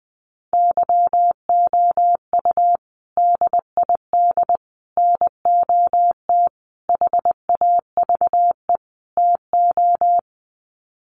Morse code, heard as a keyed tone, about 20 wpm, 700 Hz. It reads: YOU DID NOT HAVE TO
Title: Sets of 5 Words Encore 20wpm